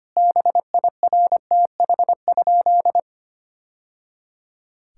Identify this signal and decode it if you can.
Morse code, heard as a keyed tone, about 25 wpm, 700 Hz. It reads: BIRTH?
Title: Sentences from Top 1000 Words - Review Part 6 - 25wpm